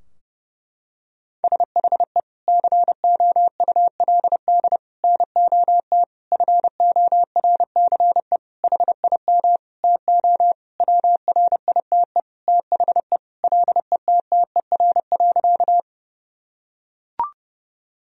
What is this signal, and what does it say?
Morse code, heard as a keyed tone, about 30 wpm, 700 Hz. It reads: SHE COULD NOT FORCE HIM TO WRITE THE LETTER.
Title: Sentences from Top 1000 Words - Review Part 3 - 30wpm